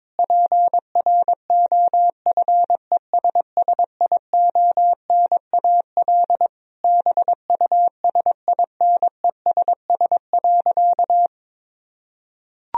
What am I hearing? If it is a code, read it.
Morse code, heard as a keyed tone, about 22 wpm, 700 Hz. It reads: PROFESSIONAL BUSINESS.
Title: Sentences from Top 1000 Words - Review Part 4 - 22wpm